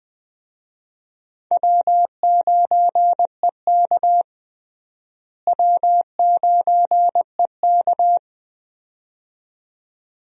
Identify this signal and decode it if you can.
Morse code, heard as a keyed tone, about 20 wpm, 700 Hz. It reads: W9EK W9EK
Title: USA Call Signs - 4 Characters - Repeated 2x - 20wpm